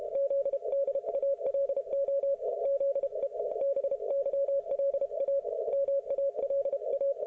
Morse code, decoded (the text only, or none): ZDALO ZE BYLA MALA